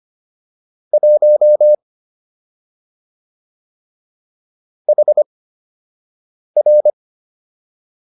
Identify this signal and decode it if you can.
Morse code, heard as a keyed tone, about 25 wpm, spaced as for 6 wpm, 600 Hz. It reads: 1 HR